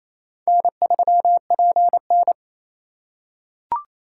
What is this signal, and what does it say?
Morse code, heard as a keyed tone, about 28 wpm, 700 Hz. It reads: N3PN
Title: USA Call Signs - 4 Characters - 8x Character Spacing - 28wpm